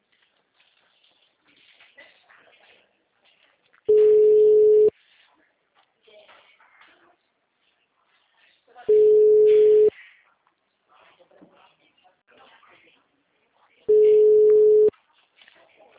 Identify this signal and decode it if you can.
Ringback tone (Europe, 425 Hz), with speech.
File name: out- -2049- - - .54000